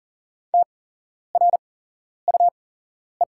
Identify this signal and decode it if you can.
Morse code, heard as a keyed tone, about 40 wpm, spaced as for 11 wpm, 700 Hz. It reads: TRUE
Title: 8x Character Spacing - Top 500 Words - 40wpm